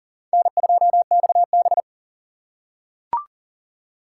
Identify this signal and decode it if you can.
Morse code, heard as a keyed tone, about 40 wpm, 700 Hz. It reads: N2XB
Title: USA Call Signs - 3x Character Spacing - 40wpm